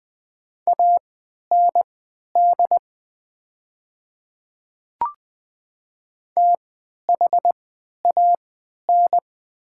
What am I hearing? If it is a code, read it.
Morse code, heard as a keyed tone, about 20 wpm, spaced as for 11 wpm, 700 Hz. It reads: AND THAN
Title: 3x Character Spacing - Top 200 Words - 20wpm